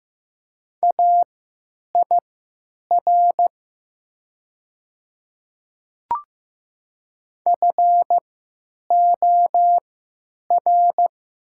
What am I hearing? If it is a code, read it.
Morse code, heard as a keyed tone, about 15 wpm, spaced as for 9 wpm, 700 Hz. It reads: AIR FOR
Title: 3x Character Spacing - Top 200 Words - 15wpm